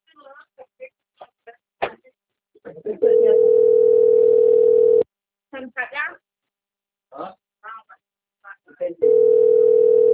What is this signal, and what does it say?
Ringback tone (North America), 440 + 480 Hz, with speech.
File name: out- - - - - .12013